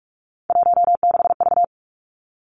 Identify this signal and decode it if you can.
Morse code, heard as a keyed tone, about 46 wpm, 700 Hz. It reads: 16V
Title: Three Challenging Letter-Number Combinations 45wpm